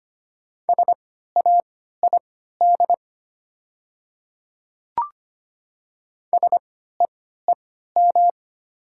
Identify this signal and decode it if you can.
Morse code, heard as a keyed tone, about 25 wpm, spaced as for 14 wpm, 700 Hz. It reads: SAID SEEM
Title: 3x Character Spacing - Top 300 Words - 25wpm